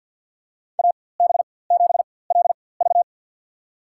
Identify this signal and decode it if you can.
Morse code, heard as a keyed tone, about 53 wpm, spaced as for 24 wpm, 700 Hz. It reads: AB7LV
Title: USA Call Signs - 5 Characters - 4x Character Spacing - 50wpm